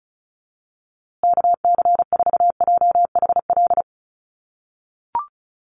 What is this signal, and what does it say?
Morse code, heard as a keyed tone, about 34 wpm, 700 Hz. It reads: KC4JHL
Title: USA Call Signs Repeated 2x 35wpm